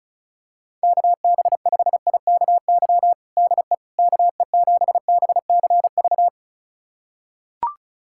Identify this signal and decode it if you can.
Morse code, heard as a keyed tone, about 36 wpm, 700 Hz. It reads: KB5IKY DE KE7BCV
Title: USA Callsign DE Callsign - 6 Characters - 35wpm